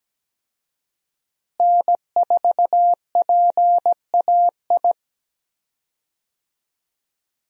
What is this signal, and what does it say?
Morse code, heard as a keyed tone, about 17 wpm, 700 Hz. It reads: N4PAI
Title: USA Call Signs - 5 Characters - Repeated 2x - 17wpm